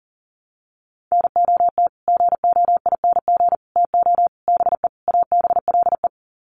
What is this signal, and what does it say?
Morse code, heard as a keyed tone, about 40 wpm, 700 Hz. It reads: NOT GOING TO BE ABLE
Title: Sets of 5 Words 40wpm